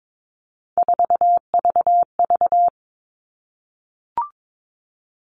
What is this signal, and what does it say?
Morse code, heard as a keyed tone, about 22 wpm, 700 Hz. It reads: 4VV
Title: All Cut Numbers - 3-Digit 22wpm